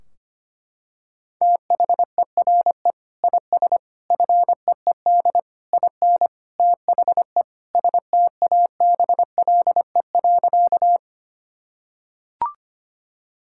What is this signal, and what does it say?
Morse code, heard as a keyed tone, about 25 wpm, 700 Hz. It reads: THERE IS FEED IN THE STABLE.
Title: Sentences from Top 1600 Words 25wpm